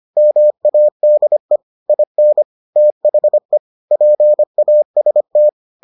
Morse code, heard as a keyed tone, about 25 wpm, 600 Hz. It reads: MADE IN THE PAST